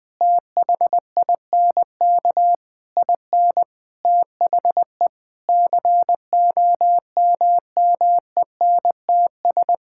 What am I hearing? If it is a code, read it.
Morse code, heard as a keyed tone, about 20 wpm, 700 Hz. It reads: THINK IN THE COMMENTS
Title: Sentences with 9-Word Phrases - 20wpm